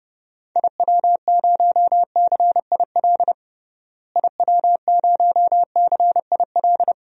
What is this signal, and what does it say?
Morse code, heard as a keyed tone, about 30 wpm, 700 Hz. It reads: IW0CIL IW0CIL
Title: DXCC-100 Call Signs - 6 Characters - Repeated 2x - 30wpm